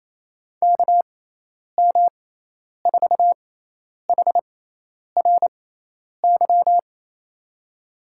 Morse code, heard as a keyed tone, about 28 wpm, spaced as for 10 wpm, 700 Hz. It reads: KM4HRY
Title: USA Call Signs - 6x Character Spacing - 28wpm